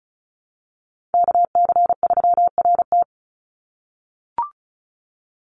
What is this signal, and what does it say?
Morse code, heard as a keyed tone, about 34 wpm, 700 Hz. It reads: KC3RT
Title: NA SOTA Chaser Callsigns 35wpm